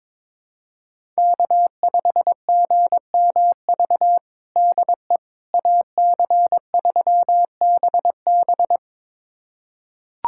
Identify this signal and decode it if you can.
Morse code, heard as a keyed tone, about 22 wpm, 700 Hz. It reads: K5GMV DE AC3BB
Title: USA Callsign DE Callsign - 5 Characters - 22wpm